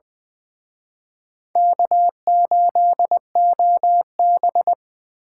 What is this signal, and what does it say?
Morse code, heard as a keyed tone, about 23 wpm, 700 Hz. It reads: K8OB
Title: Speed Racing - USA 1x2 Call Signs 20wpm